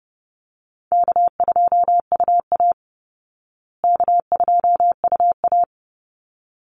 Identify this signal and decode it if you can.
Morse code, heard as a keyed tone, about 30 wpm, 700 Hz. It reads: K2UA K2UA
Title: Call Signs Repeated 3x - 30wpm